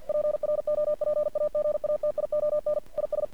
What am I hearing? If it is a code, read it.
Morse code, heard as a keyed tone, about 50 wpm, 600 Hz. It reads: PROPAGATION IS
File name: archived / CRSnet